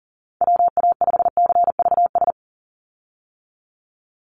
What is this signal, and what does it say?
Morse code, heard as a keyed tone, about 40 wpm, 700 Hz. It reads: WA5CVS
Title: USA Call Signs 40wpm